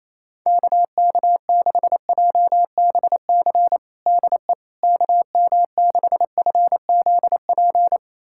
Morse code, heard as a keyed tone, about 28 wpm, 700 Hz. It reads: KK6JBC DE KM6FZP
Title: USA Callsign DE Callsign - 6 Characters - 28wpm